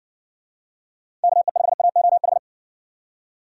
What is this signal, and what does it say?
Morse code, heard as a keyed tone, about 57 wpm, 700 Hz. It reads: KE5AOH